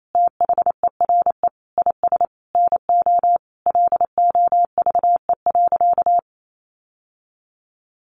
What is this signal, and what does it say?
Morse code, heard as a keyed tone, about 28 wpm, 700 Hz. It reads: THERE IS NO LOVE.
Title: Sentences from Top 1200 Words 28wpm